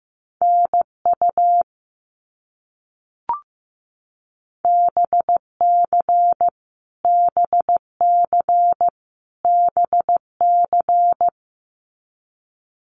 Morse code, heard as a keyed tone, about 15 wpm, 700 Hz. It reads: NU BC BC BC
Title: Canadian Provinces and Territories - Repeated 3x - 15wpm